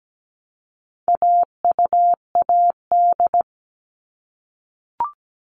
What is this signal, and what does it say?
Morse code, heard as a keyed tone, about 17 wpm, 700 Hz. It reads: AUAD
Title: All Cut Numbers - 4-Digit 17wpm